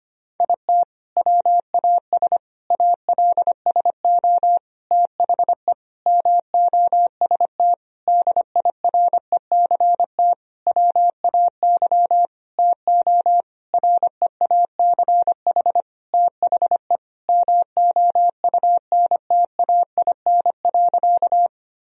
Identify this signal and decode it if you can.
Morse code, heard as a keyed tone, about 25 wpm, 700 Hz. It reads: IT WAS ALSO THE MOST DIRECT WAY TO REACH THE MOUNTAIN.
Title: Sentences from Top 1000 Words - Review Part 2 - 25wpm